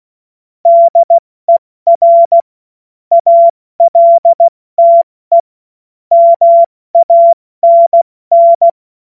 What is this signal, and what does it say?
Morse code, heard as a keyed tone, about 16 wpm, spaced as for 14 wpm, 680 Hz. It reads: DER ALTE MANN
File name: Morsekurs / Witze / Witz 23. Dezember 2025 70 F80